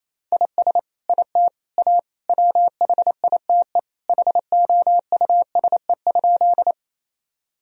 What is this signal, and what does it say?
Morse code, heard as a keyed tone, about 28 wpm, 700 Hz. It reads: IS IT A WHITE HOUSE?
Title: Sentences from Top 1000 Words - Review Part 1 - 28wpm